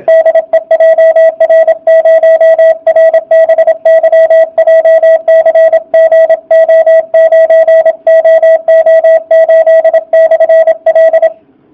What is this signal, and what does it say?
Morse code, heard as a keyed tone, about 27 wpm, 660 Hz, with radio noise. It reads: DE JR0RBY JCG O9OO8/L
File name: jh8cbh / 2019cw / sp32- w